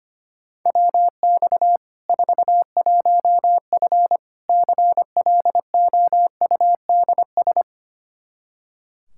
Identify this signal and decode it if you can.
Morse code, heard as a keyed tone, about 25 wpm, 700 Hz. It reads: WX 41F CLOUDS
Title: Practice QSO - Exchange 2 - Info - 25wpm